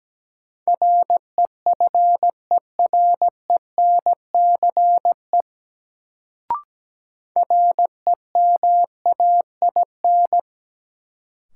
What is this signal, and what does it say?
Morse code, heard as a keyed tone, about 17 wpm, 700 Hz. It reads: REFERENCE REMAIN
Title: Sentences from Top 1200 Words - New Word Drill Alt - 17wpm